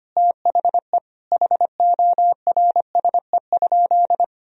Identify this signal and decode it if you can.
Morse code, heard as a keyed tone, about 25 wpm, 700 Hz. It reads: THE HORSE?
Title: Sentences from Top 500 Words and Abbreviations 25wpm